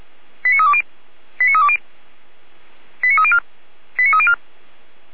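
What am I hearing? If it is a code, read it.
Five-tone selective call (ZVEI), heard as quick runs of five tones, tones of about 80 ms. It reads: 89320 89320 89304 89304
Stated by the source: ZVEI five-tone call 89294 received by SDR.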